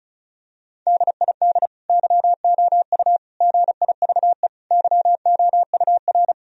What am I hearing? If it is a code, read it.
Morse code, heard as a keyed tone, about 36 wpm, 700 Hz. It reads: DID YOU GIVE YOUR
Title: Sentences from Top 1000 Words - Review Part 4 - 35wpm